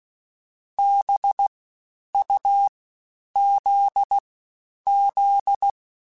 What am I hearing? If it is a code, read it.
Morse code, heard as a keyed tone, about 16 wpm, spaced as for 9 wpm, 800 Hz. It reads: BUZZ
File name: Special INGs-mp3s / Special INGs-mp3s / Fast ing @15